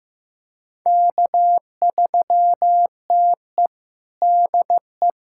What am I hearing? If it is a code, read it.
Morse code, heard as a keyed tone, about 15 wpm, 700 Hz. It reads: K3TE DE
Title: USA Callsign DE Callsign - 4 Characters - 15wpm